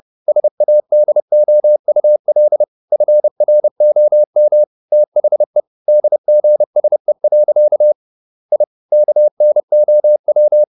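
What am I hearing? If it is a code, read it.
Morse code, heard as a keyed tone, about 30 wpm, 600 Hz. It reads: SADOUL FROM THE DGSE. I KNOW
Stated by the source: Morse code at 30 wpm, the novel Never by Ken Follett in English, from CW med Gnister.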